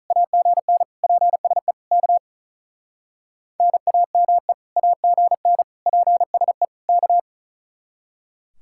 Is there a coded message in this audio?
Yes: Morse code from 0.1 to 7.2 s, heard as a keyed tone, about 45 wpm, 700 Hz. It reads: AGN PSE K NAME AGN PSE K